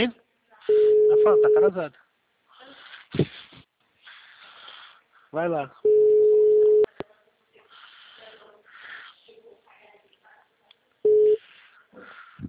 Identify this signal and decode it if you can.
Ringback tone (Europe, 425 Hz), with speech.